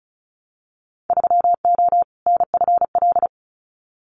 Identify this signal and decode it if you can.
Morse code, heard as a keyed tone, about 34 wpm, 700 Hz. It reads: 3O NFL
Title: Winter Field Day Exchanges 35wpm